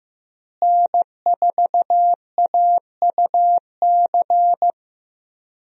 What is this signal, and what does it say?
Morse code, heard as a keyed tone, about 15 wpm, 700 Hz. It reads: N4AUC
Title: USA Call Signs - 5 Characters - Repeated 2x - 15wpm